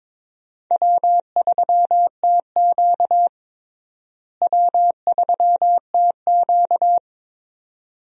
Morse code, heard as a keyed tone, about 22 wpm, 700 Hz. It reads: W3TQ W3TQ
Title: USA Call Signs - 4 Characters - Repeated 2x - 22wpm